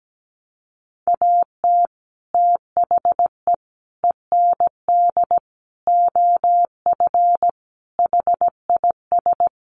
Morse code, heard as a keyed tone, about 17 wpm, 700 Hz. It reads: AT THE END OF HIS
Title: Sets of 5 Words Encore 17wpm